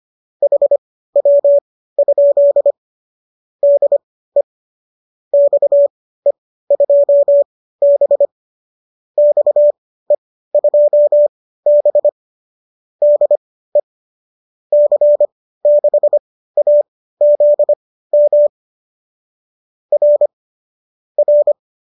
Morse code, heard as a keyed tone, about 25 wpm, spaced as for 15 wpm, 570 Hz. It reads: HW? DE XE2B XE2B DE C6AZM R R
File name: qso204 15